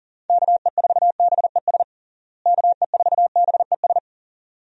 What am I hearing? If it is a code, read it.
Morse code, heard as a keyed tone, about 37 wpm, 700 Hz. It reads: KE4BES KE4BES
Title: USA Call Signs - 6 Characters - Repeated 2x - 40wpm